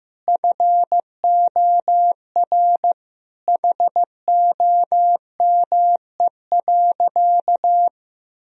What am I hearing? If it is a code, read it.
Morse code, heard as a keyed tone, about 16 wpm, 700 Hz. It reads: FOR HOME.
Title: Sentences from Top 1000 Words - Review Part 1 - 15wpm